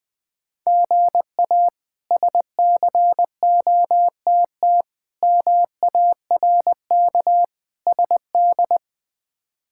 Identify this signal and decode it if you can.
Morse code, heard as a keyed tone, about 20 wpm, 700 Hz. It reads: GA SCOTT MARK SD
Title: K1USN SST QSO Practice - 20wpm